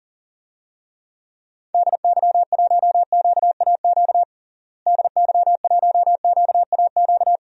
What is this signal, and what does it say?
Morse code, heard as a keyed tone, about 40 wpm, 700 Hz. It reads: DY1QAQ DY1QAQ
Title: DXCC-100 Call Signs - 6 Characters - Repeated 2x - 40wpm